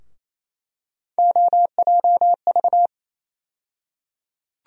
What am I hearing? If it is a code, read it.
Morse code, heard as a keyed tone, about 28 wpm, 700 Hz. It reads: OJV
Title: USA Call Sign Postfixes - 3 Letters 28wpm